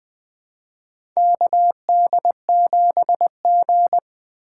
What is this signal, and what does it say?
Morse code, heard as a keyed tone, about 21 wpm, 700 Hz. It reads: KD7G
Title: USA Call Signs - 4 Characters - 3x Character Spacing - 20wpm